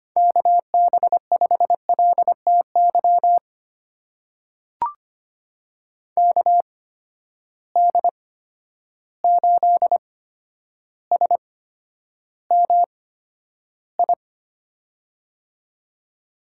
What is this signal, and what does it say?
Morse code, heard as a keyed tone, about 25 wpm, spaced as for 11 wpm, 700 Hz. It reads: KB5LTY KD8SMI
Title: USA Call Signs - 6 Characters - 8x Character Spacing - 25wpm